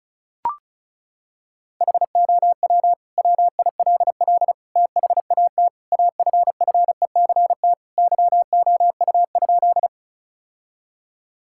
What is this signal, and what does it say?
Morse code, heard as a keyed tone, about 34 wpm, 700 Hz. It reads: HOW WILL THAT AFFECT YOU?
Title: Sentences from Top 1000 Words - Review Part 6 - 35wpm